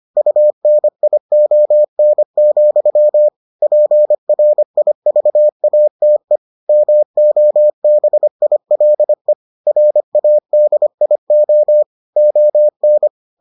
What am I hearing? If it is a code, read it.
Morse code, heard as a keyed tone, about 25 wpm, 600 Hz. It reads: UNION, PRIVATE MOBILE RADIO ON